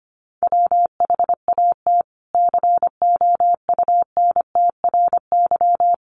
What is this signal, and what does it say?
Morse code, heard as a keyed tone, about 25 wpm, 700 Hz. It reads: WHAT COUNTRY
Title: Sentences from Top 300 Words 25wpm